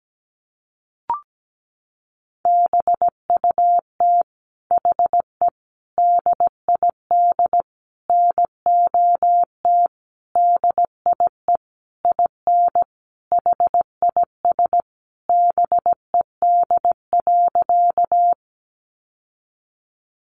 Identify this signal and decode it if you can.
Morse code, heard as a keyed tone, about 17 wpm, 700 Hz. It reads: BUT HE DID NOT DIE IN HIS BED.
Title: Sentences from Top 1300 Words 17wpm